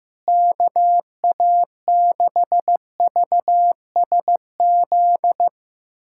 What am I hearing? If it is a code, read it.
Morse code, heard as a keyed tone, about 15 wpm, 700 Hz. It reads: KA6VSZ
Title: USA Callsign DE Callsign - 6 Characters - 15wpm